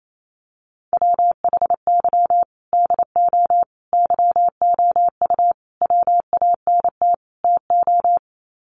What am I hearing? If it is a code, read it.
Morse code, heard as a keyed tone, about 28 wpm, 700 Hz. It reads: WHY DO YOU WANT TO
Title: Sets of 5 Words Encore2 28wpm